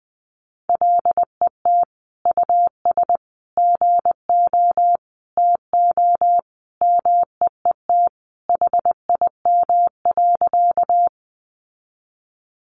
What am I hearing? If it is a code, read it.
Morse code, heard as a keyed tone, about 19 wpm, 700 Hz. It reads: LET US GO TO MEET HIM.
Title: Sentences from Top 1100 Words 20wpm